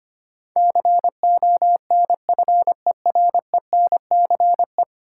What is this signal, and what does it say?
Morse code, heard as a keyed tone, about 25 wpm, 700 Hz. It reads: CONFERENCE
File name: Ten Letter Words - 25wpm